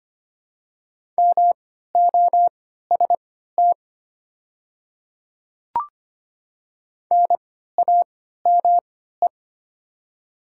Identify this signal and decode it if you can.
Morse code, heard as a keyed tone, about 25 wpm, spaced as for 14 wpm, 700 Hz. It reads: MOST NAME
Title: 3x Character Spacing - Top 200 Words - 25wpm